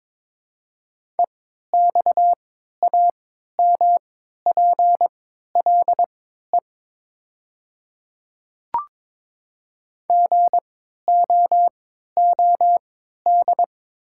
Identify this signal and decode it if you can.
Morse code, heard as a keyed tone, about 22 wpm, spaced as for 13 wpm, 700 Hz. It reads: EXAMPLE GOOD
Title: 3x Character Spacing - Top 300 Words - 22wpm